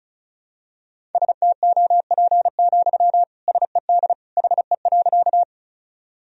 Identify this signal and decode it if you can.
Morse code, heard as a keyed tone, about 36 wpm, 700 Hz. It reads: STOP, SED HE.
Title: Sentences from Top 300 Words and Abbreviations 35wpm